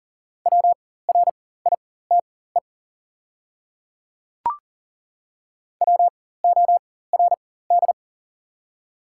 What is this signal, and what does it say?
Morse code, heard as a keyed tone, about 40 wpm, spaced as for 19 wpm, 690 Hz. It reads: WRITE WORD
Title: 4x Character Spacing - Top 200 Words - 40wpm